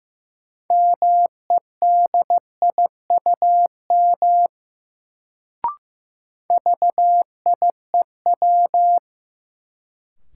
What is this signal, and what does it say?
Morse code, heard as a keyed tone, about 15 wpm, 700 Hz. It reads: MEDIUM VIEW
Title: Sentences from Top 900 Words - New Word Drill - 15wpm